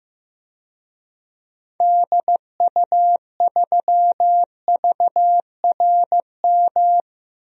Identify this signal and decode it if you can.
Morse code, heard as a keyed tone, about 15 wpm, 700 Hz. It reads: DU3VRM
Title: DXCC-100 Call Signs - 6 Characters - 15wpm